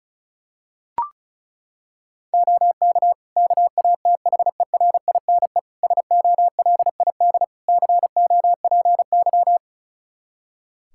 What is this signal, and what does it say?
Morse code, heard as a keyed tone, about 36 wpm, 700 Hz. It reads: OK KATHERINE SOLID COPY
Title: Practice QSO - Exchange 3 - More Info - 35wpm